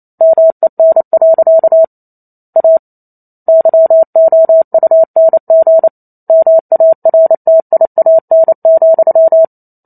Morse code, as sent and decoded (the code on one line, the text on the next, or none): -- . -. .-.-.- / .- / -.-- --- ..- -. --. / -- .- .-. - .. .- -. --..--
MEN. A YOUNG MARTIAN,